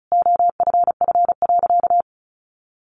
Morse code, heard as a keyed tone, about 34 wpm, 700 Hz. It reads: OFF.
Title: Sentences from Top 1100 Words 35wpm